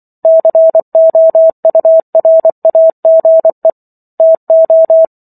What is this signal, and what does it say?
Morse code, heard as a keyed tone, about 24 wpm, 650 Hz. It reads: COURAGE TO
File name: Chapter 15 - 24 WPM